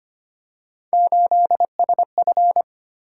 Morse code, heard as a keyed tone, about 25 wpm, 700 Hz. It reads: 8SF
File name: Three Letter-Number Combinations - 25wpm